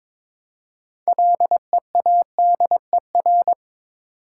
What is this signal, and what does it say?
Morse code, heard as a keyed tone, about 22 wpm, 700 Hz. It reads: LEADER
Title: Six Letter Words - 22wpm